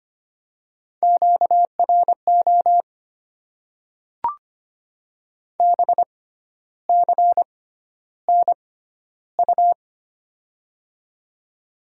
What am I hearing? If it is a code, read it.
Morse code, heard as a keyed tone, about 25 wpm, spaced as for 12 wpm, 700 Hz. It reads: QRO BCNU
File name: Top 90 QSO Elements - 6x Character Spacing - 25wpm